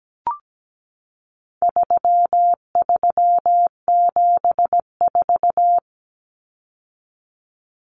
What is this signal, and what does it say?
Morse code, heard as a keyed tone, about 17 wpm, 700 Hz. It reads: 3374
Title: Numbers - 4-Digit 17wpm